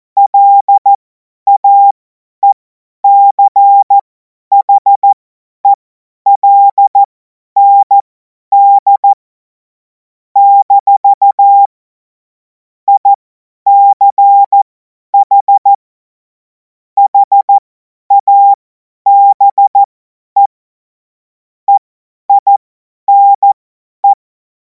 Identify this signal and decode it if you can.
Morse code, heard as a keyed tone, about 14 wpm, spaced as for 10 wpm, 800 Hz. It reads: LAECHELND - ICH HABE EINE
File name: Morsekurs / Witze / Witz 14. Juli 2026 50 F70